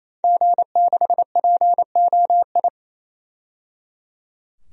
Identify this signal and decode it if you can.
Morse code, heard as a keyed tone, about 27 wpm, 700 Hz. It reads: G6POI
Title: DXCC-100 Call Signs - 5 Characters - 28wpm